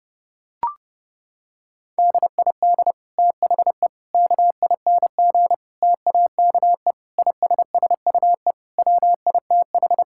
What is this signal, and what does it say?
Morse code, heard as a keyed tone, about 32 wpm, 700 Hz. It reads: DID THE KING TAKE ISSUE WITH
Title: Sentences from Top 1000 Words - Review Part 6 - 30wpm